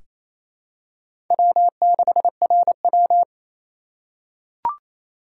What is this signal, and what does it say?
Morse code, heard as a keyed tone, about 28 wpm, 700 Hz. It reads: W6RW
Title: USA Call Signs - 4 Characters - 3x Character Spacing - 28wpm